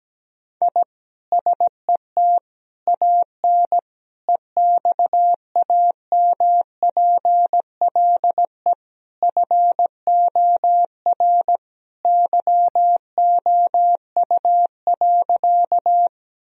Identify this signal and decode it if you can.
Morse code, heard as a keyed tone, about 17 wpm, 700 Hz. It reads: I SET AN EXAMPLE FOR YOU.
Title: Sentences from Top 1000 Words - Review Part 1 - 17wpm